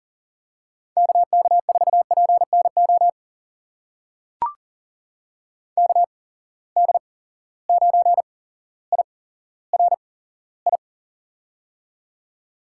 Morse code, heard as a keyed tone, about 38 wpm, spaced as for 17 wpm, 700 Hz. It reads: KK4PNO KD9IRI
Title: USA Call Signs - 6 Characters - 8x Character Spacing - 40wpm